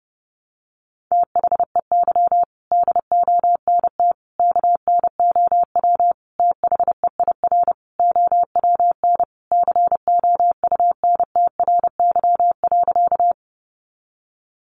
Morse code, heard as a keyed tone, about 30 wpm, 700 Hz. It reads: THEY DONT KNOW THEIR OWN COUNTRY.
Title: Sentences from Top 300 Words 30wpm